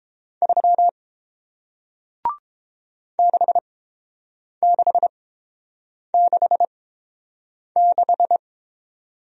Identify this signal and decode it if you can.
Morse code, heard as a keyed tone, about 30 wpm, spaced as for 7 wpm, 700 Hz. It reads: 3 6666